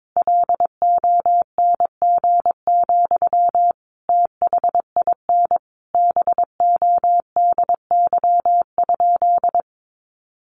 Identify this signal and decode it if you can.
Morse code, heard as a keyed tone, about 22 wpm, 700 Hz. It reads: LONG, THIN BODY?
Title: Sentences from Top 1200 Words 22wpm